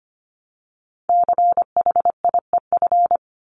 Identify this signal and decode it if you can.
Morse code, heard as a keyed tone, about 25 wpm, 700 Hz. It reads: CHIEF
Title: LY Suffix - Top 100 Words with Base Words 25wpm